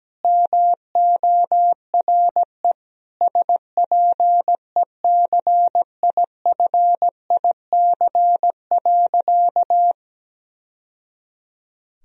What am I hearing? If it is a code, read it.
Morse code, heard as a keyed tone, about 17 wpm, 700 Hz. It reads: MORE SPECIFIC.
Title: Sentences from Top 1000 Words - Review Part 3 - 17wpm